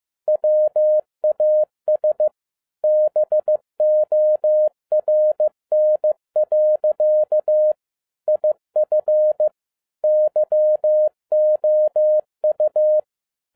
Morse code, heard as a keyed tone, about 15 wpm, 610 Hz. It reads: WAS BORN. IF YOU
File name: 2011-10-mp3 / n12-JL3YMV 15wpm